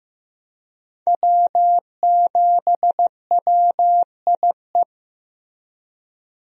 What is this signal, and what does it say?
Morse code, heard as a keyed tone, about 16 wpm, 700 Hz. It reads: W7WIE
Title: USA Call Signs - 5 Characters - 15wpm